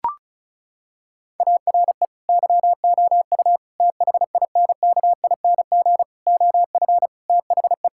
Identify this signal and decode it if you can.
Morse code, heard as a keyed tone, about 36 wpm, 700 Hz. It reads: ARE YOU THINKING OF THE
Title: Sentences from Top 1000 Words - Review Part 6 - 35wpm